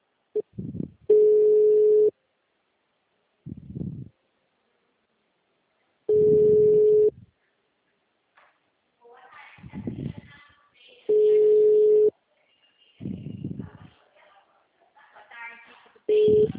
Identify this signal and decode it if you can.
Ringback tone (Europe, 425 Hz), with speech.